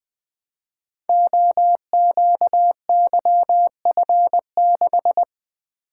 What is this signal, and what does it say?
Morse code, heard as a keyed tone, about 20 wpm, 700 Hz. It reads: OQYF6
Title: Code Groups - 5 Letter:Numbers - 20wpm